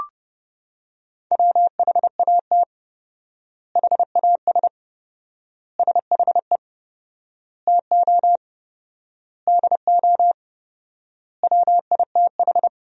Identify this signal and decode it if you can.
Morse code, heard as a keyed tone, about 32 wpm, 700 Hz. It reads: WHAT HAS SHE TO DO WITH
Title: 4x Word-Spacing - Sentences from Top 100 Words - 30wpm